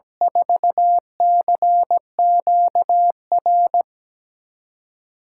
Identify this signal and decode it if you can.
Morse code, heard as a keyed tone, about 17 wpm, 700 Hz. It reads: 4CQR
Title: USA Callsign DE Callsign 17wpm